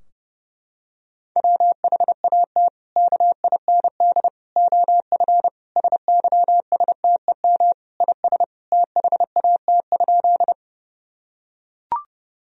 Morse code, heard as a keyed tone, about 30 wpm, 700 Hz. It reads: WHAT KIND OF SYSTEM IS THAT?
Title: Sentences from Top 1000 Words - Review Part 3 - 30wpm